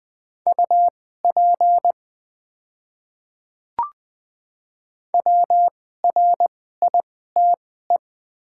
Morse code, heard as a keyed tone, about 20 wpm, spaced as for 15 wpm, 700 Hz. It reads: UP WRITE
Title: 2x Character Spacing - Top 100 Words - 20wpm